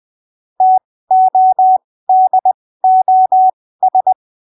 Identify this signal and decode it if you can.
Morse code, heard as a keyed tone, about 20 wpm, spaced as for 15 wpm, 750 Hz. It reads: TODOS
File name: arquivos / texto / texto 25 20ppm